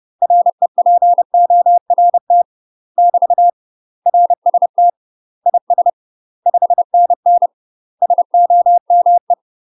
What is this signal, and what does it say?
Morse code, heard as a keyed tone, about 30 wpm, 700 Hz. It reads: REPORT <BT> RST IS 5NN SOME
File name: UK MASTER QSOs TRAINING / 30WPM / UK MASTER QSOs 0039